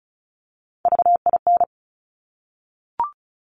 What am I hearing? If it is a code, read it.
Morse code, heard as a keyed tone, about 36 wpm, 700 Hz. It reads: VIN
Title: Top Three Letter Combinations in Words 35wpm